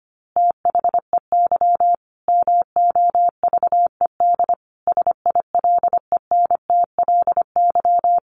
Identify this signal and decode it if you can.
Morse code, heard as a keyed tone, about 25 wpm, 700 Hz. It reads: THEY MOVED SILENTLY
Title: A Sound Of Thunder 25wpm